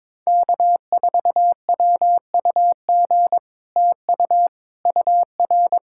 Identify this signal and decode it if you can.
Morse code, heard as a keyed tone, about 22 wpm, 700 Hz. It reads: K4WUG TU UR
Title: Speed-Racing Common QSO Phrases - 22wpm